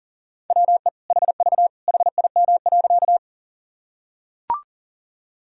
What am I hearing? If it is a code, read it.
Morse code, heard as a keyed tone, about 40 wpm, 700 Hz. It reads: WE HV HIM.
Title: Sentences from Top 100 Words and Abbreviations 40wpm